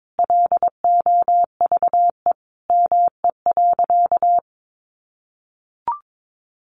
Morse code, heard as a keyed tone, about 22 wpm, 700 Hz. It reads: LOVE ME.
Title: Sentences from Top 400 Words 22wpm